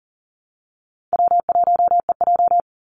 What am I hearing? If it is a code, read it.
Morse code, heard as a keyed tone, about 40 wpm, 700 Hz. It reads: W1EJ
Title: NA SOTA Activator Callsigns 40wpm